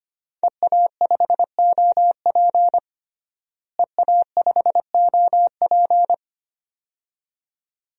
Morse code, heard as a keyed tone, about 25 wpm, 700 Hz. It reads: EA5OP EA5OP
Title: DXCC-100 Call Signs - 5 Characters - Repeated 2x - 25wpm